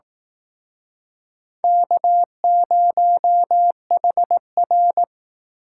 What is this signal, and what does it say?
Morse code, heard as a keyed tone, about 18 wpm, 700 Hz. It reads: K0HR
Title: Speed Racing - USA 1x2 Call Signs 15wpm